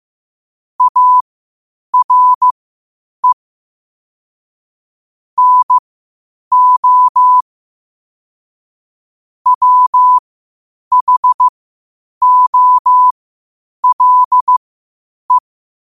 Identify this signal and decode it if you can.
Morse code, heard as a keyed tone, about 15 wpm, spaced as for 8 wpm, 1000 Hz. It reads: ARE NO WHOLE
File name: qotdfiles / 8wpm